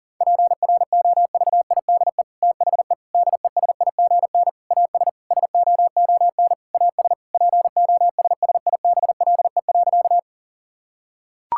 Morse code, heard as a keyed tone, about 40 wpm, 700 Hz. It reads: PROVIDE THE DESIGN AS SOON AS POSSIBLE.
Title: Sentences from Top 1000 Words - Review Part 5 - 40wpm